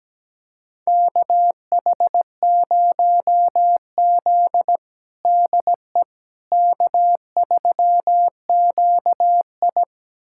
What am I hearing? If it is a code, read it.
Morse code, heard as a keyed tone, about 17 wpm, 700 Hz. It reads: KH0Z DE K3QI
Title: USA Callsign DE Callsign - 4 Characters - 17wpm